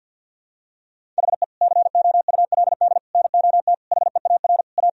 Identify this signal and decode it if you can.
Morse code, heard as a keyed tone, about 51 wpm, 700 Hz. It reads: HE COULD NOT HEAR A
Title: Sentences from Top 1000 Words - Review Part 1 - 50wpm